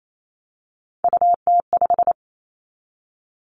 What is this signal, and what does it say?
Morse code, heard as a keyed tone, about 28 wpm, 700 Hz. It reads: UT5
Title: Callsign Prefixes for Top 25 DXCC - 28wpm